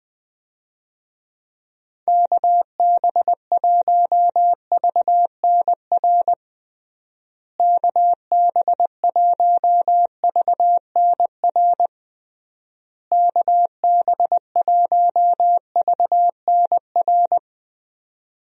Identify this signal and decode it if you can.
Morse code, heard as a keyed tone, about 20 wpm, 700 Hz. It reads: KB1VNR KB1VNR KB1VNR
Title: USA Call Signs - 6 Characters - Repeated 3x - 20wpm